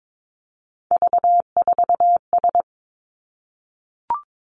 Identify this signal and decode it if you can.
Morse code, heard as a keyed tone, about 22 wpm, 700 Hz. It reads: V4S
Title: Three Challenging Letter-Number Combinations 22wpm